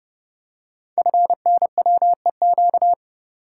Morse code, heard as a keyed tone, about 30 wpm, 700 Hz. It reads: FNWEQ
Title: Code Groups - 5 Letters - 30wpm